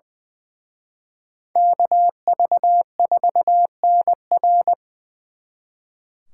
Morse code, heard as a keyed tone, about 24 wpm, 700 Hz. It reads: KV4NR
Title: Speed Racing - USA 2x2 Call Signs 20wpm